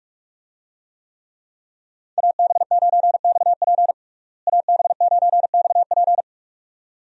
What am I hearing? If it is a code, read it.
Morse code, heard as a keyed tone, about 42 wpm, 700 Hz. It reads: AB9XP AB9XP
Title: USA Call Signs - 5 Characters - Repeated 2x - 45wpm